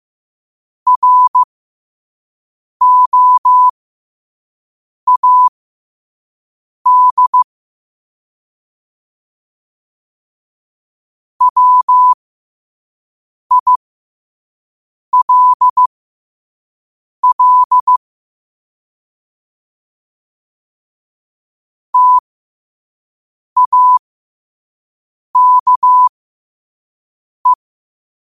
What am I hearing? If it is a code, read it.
Morse code, heard as a keyed tone, about 15 wpm, spaced as for 5 wpm, 1000 Hz. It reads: ROAD WILL TAKE